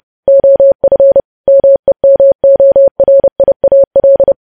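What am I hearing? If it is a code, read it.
Morse code, heard as a keyed tone, about 30 wpm, 550 Hz. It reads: OF MEMORIAL
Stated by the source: Morse code news headlines sent at 30, preceded by vvv vvv.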